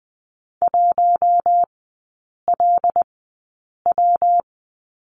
Morse code, heard as a keyed tone, about 20 wpm, spaced as for 8 wpm, 700 Hz. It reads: 1LW
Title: 114 Lesson 19e - Letter w - Three Characters 20wpm